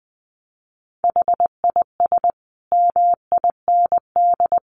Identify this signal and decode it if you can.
Morse code, heard as a keyed tone, about 20 wpm, 700 Hz. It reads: HIS MIND
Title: Sentences from Top 1500 Words 20wpm